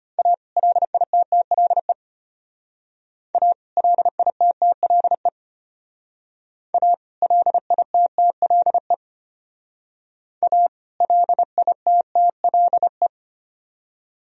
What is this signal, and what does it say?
Morse code, heard as a keyed tone, about 32 wpm, 700 Hz. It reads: A LITTLE A LITTLE A LITTLE A LITTLE